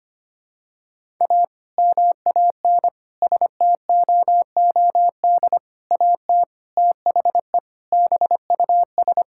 Morse code, heard as a keyed tone, about 25 wpm, 700 Hz. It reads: A MAN STOOD AT THE BUS
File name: practice / 25wpm / Sentences with 2-Word Phrases - 25wpm